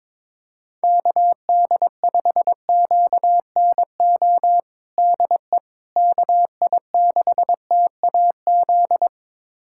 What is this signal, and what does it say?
Morse code, heard as a keyed tone, about 22 wpm, 700 Hz. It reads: KD5QNO DE KI6TAZ
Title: USA Callsign DE Callsign 22wpm